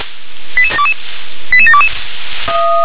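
Five-tone selective call (ZVEI), heard as quick runs of five tones, tones of about 70 ms. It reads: 88722 88722
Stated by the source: Received on SDR, ZVEI five-tone call 88697.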